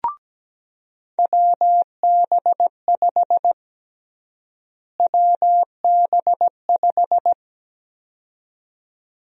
Morse code, heard as a keyed tone, about 17 wpm, 700 Hz. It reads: WB5 WB5
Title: USA Call Sign Prefixes - Repeated 2x - 17wpm